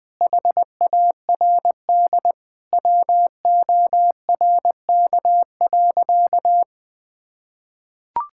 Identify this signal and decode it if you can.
Morse code, heard as a keyed tone, about 21 wpm, 700 Hz. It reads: HARD WORK.
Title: Sentences with 6-Word Phrases - 20wpm